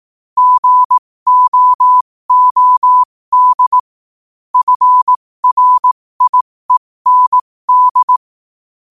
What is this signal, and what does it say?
Morse code, heard as a keyed tone, about 18 wpm, spaced as for 15 wpm, 1000 Hz. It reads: GOOD FRIEND
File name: qotdfiles / 15wpm